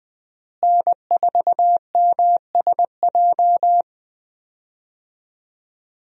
Morse code, heard as a keyed tone, about 20 wpm, 700 Hz. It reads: N4MSJ
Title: USA Call Signs - 5 Characters - Repeated 2x - 20wpm